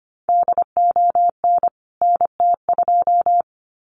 Morse code, heard as a keyed tone, about 25 wpm, 700 Hz. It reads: DON NT2
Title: CWT Exchange 25wpm